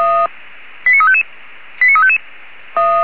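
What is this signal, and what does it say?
Five-tone selective call (ZVEI), heard as quick runs of five tones, tones of about 70 ms. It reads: 89355 89355
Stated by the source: ZVEI five-tone selective call received on SDR, fire brigade alerting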